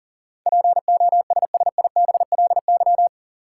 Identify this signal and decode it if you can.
Morse code, heard as a keyed tone, about 40 wpm, 700 Hz. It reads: POSSIBLY
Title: Sentences from Top 1300 Words - New Word Drill - 40wpm